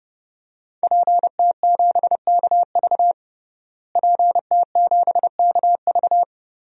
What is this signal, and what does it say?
Morse code, heard as a keyed tone, about 29 wpm, 700 Hz. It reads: PT7KV PT7KV
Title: DXCC-100 Call Signs - 5 Characters - Repeated 3x - 30wpm